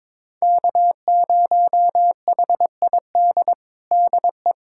Morse code, heard as a keyed tone, about 22 wpm, 700 Hz. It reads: K0HID DE
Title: USA Callsign DE Callsign - 5 Characters - 22wpm